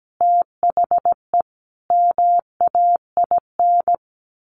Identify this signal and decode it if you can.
Morse code, heard as a keyed tone, about 17 wpm, 700 Hz. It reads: THE MAIN
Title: Sentences from Top 1300 Words 17wpm